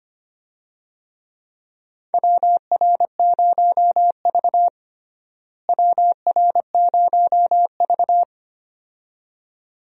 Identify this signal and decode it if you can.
Morse code, heard as a keyed tone, about 25 wpm, 700 Hz. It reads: WR0V WR0V
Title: USA Call Signs - 4 Characters - Repeated 2x - 25wpm